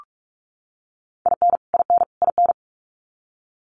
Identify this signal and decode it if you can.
Morse code, heard as a keyed tone, about 44 wpm, 700 Hz. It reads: IN IN IN
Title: US States Repeated 3x 45wpm